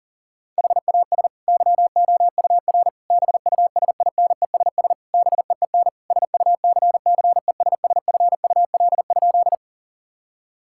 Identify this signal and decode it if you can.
Morse code, heard as a keyed tone, about 40 wpm, 700 Hz. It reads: HAS YOUR BUSINESS BEEN SUCCESSFUL?
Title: Sentences from Top 1000 Words - Review Part 6 - 40wpm